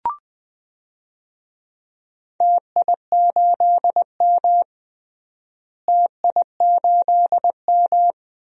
Morse code, heard as a keyed tone, about 20 wpm, 700 Hz. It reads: TI8M TI8M
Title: DXCC-100 Call Signs - 4 Characters - Repeated 2x - 20wpm